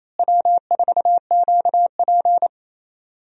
Morse code, heard as a keyed tone, about 28 wpm, 700 Hz. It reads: W4QP
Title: USA Call Signs - 4 Characters - Repeated 3x - 28wpm